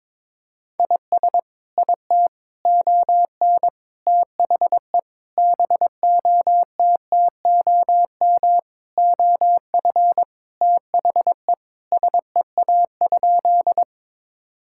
Morse code, heard as a keyed tone, about 22 wpm, 700 Hz. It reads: IS IT ON THE BOTTOM OF THE SEA?
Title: Sentences from Top 1000 Words - Review Part 6 - 22wpm